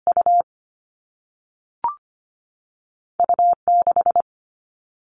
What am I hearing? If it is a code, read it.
Morse code, heard as a keyed tone, about 25 wpm, 700 Hz. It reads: U U 6
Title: All Cut Numbers - 2-Digit 25wpm